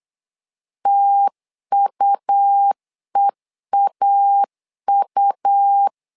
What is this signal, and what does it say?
Morse code, heard as a keyed tone, about 8 wpm, 800 Hz. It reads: TUEAU
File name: LEZIONI / 3 MWRO / 3 M 8wpm 2300